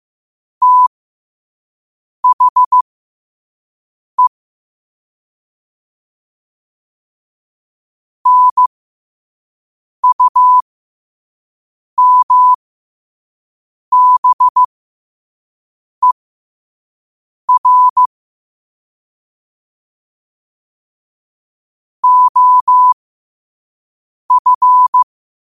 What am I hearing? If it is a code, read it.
Morse code, heard as a keyed tone, about 15 wpm, spaced as for 5 wpm, 1000 Hz. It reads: THE NUMBER OF
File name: qotdfiles / 5wpm